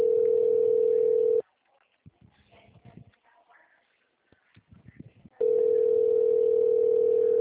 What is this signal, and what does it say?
Ringback tone (North America), 440 + 480 Hz.